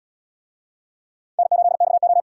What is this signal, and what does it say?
Morse code, heard as a keyed tone, about 68 wpm, 700 Hz. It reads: N8XO